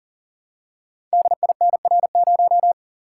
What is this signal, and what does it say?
Morse code, heard as a keyed tone, about 39 wpm, 690 Hz. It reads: DINR0
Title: Code Groups - 5 Letter:Numbers - 40wpm